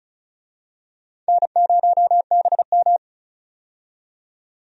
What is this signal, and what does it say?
Morse code, heard as a keyed tone, about 36 wpm, 700 Hz. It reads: N0BM
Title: USA Call Signs - 4 Characters - 35wpm